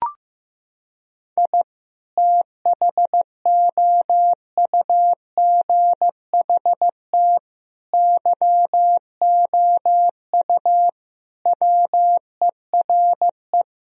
Morse code, heard as a keyed tone, about 15 wpm, 700 Hz. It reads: I THOUGHT YOU WERE
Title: Sentences from Top 1000 Words - Review Part 4 - 15wpm